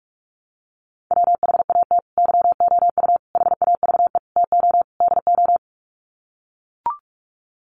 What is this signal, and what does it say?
Morse code, heard as a keyed tone, about 44 wpm, 700 Hz. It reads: WHAT YOU HAVE TO DO
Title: Sets of 5 Words Encore 45wpm